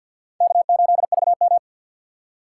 Morse code, heard as a keyed tone, about 53 wpm, 700 Hz. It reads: K8VM
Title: USA Call Signs - 4 Characters - 2x Character Spacing - 50wpm